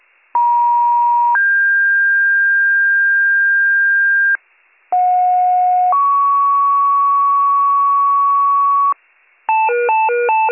Two-tone page, heard as a one-second tone, then a three-second tone, about 950 then 1600 Hz, about 730 then 1120 Hz, with noise. It is heard from 0.3 to 8.9 s.